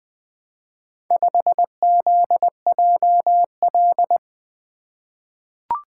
Morse code, heard as a keyed tone, about 20 wpm, 700 Hz. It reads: 5ZJL
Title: Four Letter-Number Combinations - 20wpm